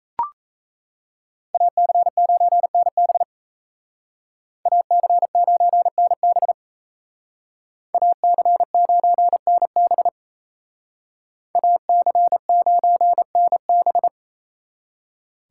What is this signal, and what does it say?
Morse code, heard as a keyed tone, about 34 wpm, 700 Hz. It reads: AC9NB AC9NB AC9NB AC9NB
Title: Speed Racing - USA 2x2 Call Signs 28wpm